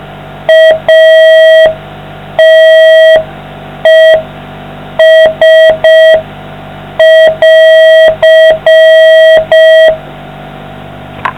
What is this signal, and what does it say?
Morse code, heard as a keyed tone, about 5 wpm, 640 Hz. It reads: ATES<AR>